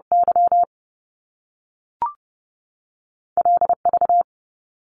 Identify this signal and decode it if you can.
Morse code, heard as a keyed tone, about 30 wpm, spaced as for 3 wpm, 700 Hz. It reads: YL V
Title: Top Two Letter Combinations in Words 30wpm